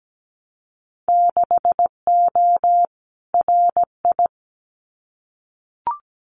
Morse code, heard as a keyed tone, about 17 wpm, 700 Hz. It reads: 6O RI
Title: Winter Field Day Exchanges Encore 17wpm